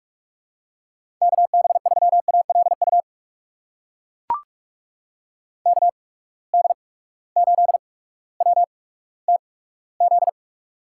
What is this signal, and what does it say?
Morse code, heard as a keyed tone, about 43 wpm, spaced as for 19 wpm, 700 Hz. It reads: KB3ALU KD8WTZ
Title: USA Call Signs - 6 Characters - 8x Character Spacing - 45wpm